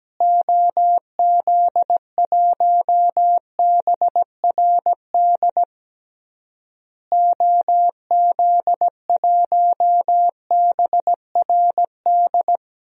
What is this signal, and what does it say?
Morse code, heard as a keyed tone, about 17 wpm, 700 Hz. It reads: OZ1BRD OZ1BRD
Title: DXCC-100 Call Signs - 6 Characters - Repeated 3x - 17wpm